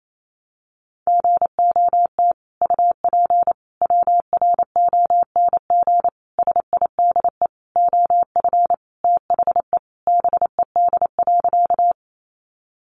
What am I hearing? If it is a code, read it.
Morse code, heard as a keyed tone, about 28 wpm, 700 Hz. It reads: GOT UP WRONG SIDE OF THE BED.
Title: Sentences from Top 900 Words 28wpm